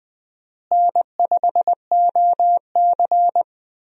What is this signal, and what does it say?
Morse code, heard as a keyed tone, about 20 wpm, 700 Hz. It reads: N5OC
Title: USA Call Signs - 4 Characters - Repeated 2x - 20wpm